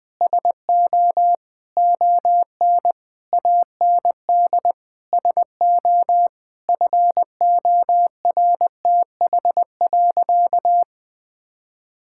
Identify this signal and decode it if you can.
Morse code, heard as a keyed tone, about 21 wpm, 700 Hz. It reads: SO ON AND SO FORTH.
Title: Sentences with 6-Word Phrases - 20wpm